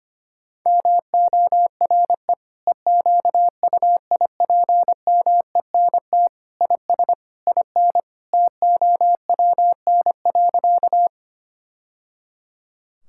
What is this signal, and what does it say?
Morse code, heard as a keyed tone, about 25 wpm, 700 Hz. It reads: MORE EQUIPMENT IS IN TOWN.
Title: Sentences from Top 1000 Words - Review Part 5 - 25wpm